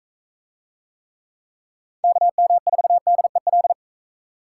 Morse code, heard as a keyed tone, about 44 wpm, 700 Hz. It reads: KM4DEL